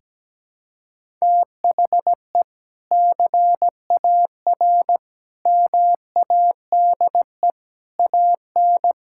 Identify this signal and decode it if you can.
Morse code, heard as a keyed tone, about 17 wpm, 700 Hz. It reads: THE CAR MADE AN
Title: Sentences from Top 1000 Words - Review Part 6 - 17wpm